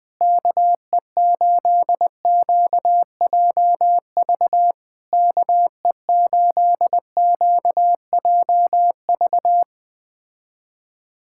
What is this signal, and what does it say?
Morse code, heard as a keyed tone, about 20 wpm, 700 Hz. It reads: KE8QJV KE8QJV
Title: POTA QSO Practice - 20wpm